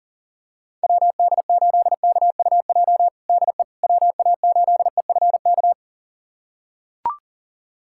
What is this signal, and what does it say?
Morse code, heard as a keyed tone, about 37 wpm, 700 Hz. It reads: WD8KUJ DE WA8EFK
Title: USA Callsign DE Callsign 40wpm